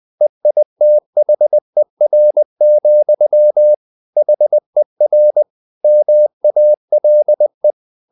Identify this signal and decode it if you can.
Morse code, heard as a keyed tone, about 20 wpm, 600 Hz. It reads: EITHER, HER MALE
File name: the old man and the sea part 2 20wpm